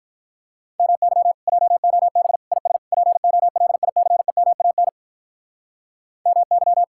Morse code, heard as a keyed tone, about 44 wpm, 700 Hz. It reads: MY JOB IS POLICEMAN MY